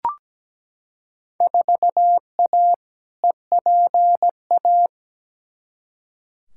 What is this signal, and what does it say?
Morse code, heard as a keyed tone, about 17 wpm, 700 Hz. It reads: 4A EPA
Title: Field Day Exchanges - 17wpm